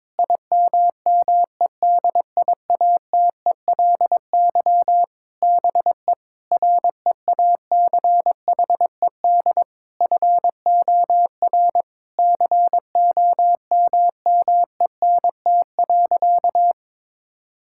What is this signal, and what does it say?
Morse code, heard as a keyed tone, about 21 wpm, 700 Hz. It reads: IMMEDIATELY BE REACHED FOR COMMENT.
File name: Sentences with 7-Word Phrases - 22wpm